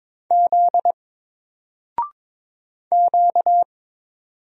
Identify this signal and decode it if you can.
Morse code, heard as a keyed tone, about 22 wpm, 700 Hz. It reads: Z Q
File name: Single-Letter-Number-22wpm